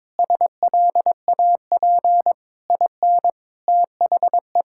Morse code, heard as a keyed tone, about 21 wpm, 700 Hz. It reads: SLAP IN THE
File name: practice / 22wpm / Sentences with 8-Word Phrases - 22wpm